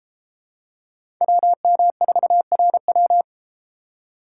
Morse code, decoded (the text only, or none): WM4RW